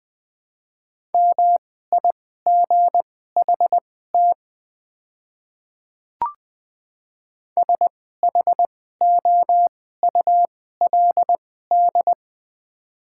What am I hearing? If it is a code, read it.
Morse code, heard as a keyed tone, about 20 wpm, spaced as for 14 wpm, 700 Hz. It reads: MIGHT SHOULD
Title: 2x Character Spacing - Top 300 Words - 20wpm